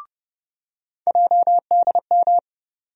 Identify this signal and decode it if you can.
Morse code, heard as a keyed tone, about 32 wpm, 700 Hz. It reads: JDM
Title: USA Call Sign Postfixes - 3 Letters 30wpm